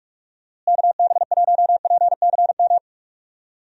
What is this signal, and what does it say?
Morse code, heard as a keyed tone, about 41 wpm, 700 Hz. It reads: KB1PCM
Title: USA Call Signs - 6 Characters - Repeated 3x - 45wpm